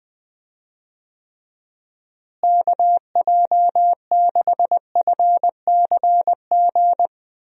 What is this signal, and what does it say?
Morse code, heard as a keyed tone, about 21 wpm, 700 Hz. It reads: KJ6FCG
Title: USA Call Signs - 6 Characters - Repeated 3x - 20wpm